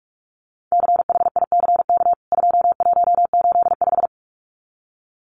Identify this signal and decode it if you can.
Morse code, heard as a keyed tone, about 44 wpm, 700 Hz. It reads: CHICK 2185
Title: CWT Exchange 45wpm